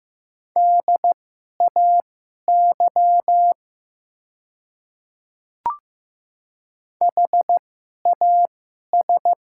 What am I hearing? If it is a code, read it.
Morse code, heard as a keyed tone, about 15 wpm, spaced as for 11 wpm, 700 Hz. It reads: DAY HAS
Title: 2x Character Spacing - Top 100 Words - 15wpm